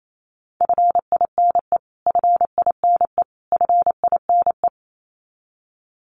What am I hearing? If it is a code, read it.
Morse code, heard as a keyed tone, about 28 wpm, 700 Hz. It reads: FINE FINE FINE
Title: Top 500 Words Repeated 28wpm